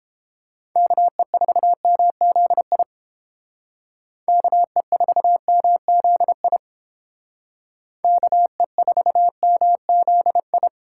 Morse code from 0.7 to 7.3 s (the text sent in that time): KE4MZI KE4MZI